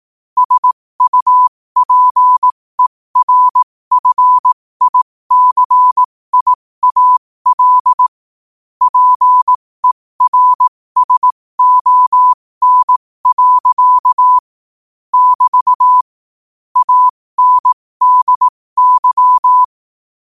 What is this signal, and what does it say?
Morse code, heard as a keyed tone, about 18 wpm, spaced as for 15 wpm, 1000 Hz. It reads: SUPERFICIAL PERSON. <BT> ANDY
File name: qotdfiles / 15wpm